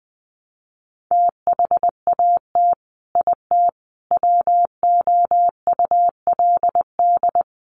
Morse code, heard as a keyed tone, about 20 wpm, 700 Hz. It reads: THAT IT WOULD
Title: Sets of 5 Words Encore2 20wpm